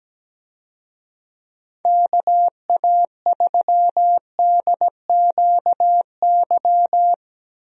Morse code, heard as a keyed tone, about 17 wpm, 700 Hz. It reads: KA3DQY
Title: USA Call Signs - 6 Characters - Repeated 3x - 17wpm